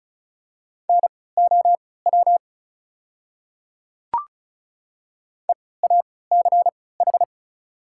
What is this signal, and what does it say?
Morse code, heard as a keyed tone, about 36 wpm, spaced as for 20 wpm, 690 Hz. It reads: NOW EACH
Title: 3x Character Spacing - Top 100 Words - 35wpm